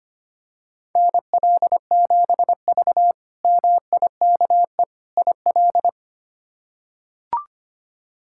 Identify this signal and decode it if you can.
Morse code, heard as a keyed tone, about 25 wpm, 700 Hz. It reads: NL7V MIKE IL
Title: K1USN SST QSO Practice - 25wpm